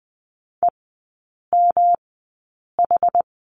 Morse code, heard as a keyed tone, about 20 wpm, spaced as for 8 wpm, 700 Hz. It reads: EMH
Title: 107 Lesson 18e - Letter m - Three Characters 20wpm